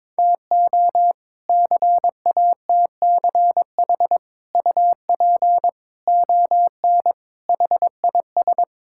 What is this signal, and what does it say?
Morse code, heard as a keyed tone, about 22 wpm, 700 Hz. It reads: TO CATCH UP ON HIS
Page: morse-code-ninja-practice-22wpm